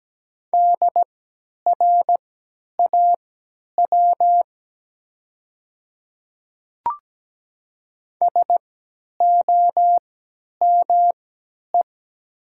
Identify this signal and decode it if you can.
Morse code, heard as a keyed tone, about 17 wpm, spaced as for 10 wpm, 700 Hz. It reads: DRAW SOME
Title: 3x Character Spacing - Top 300 Words - 17wpm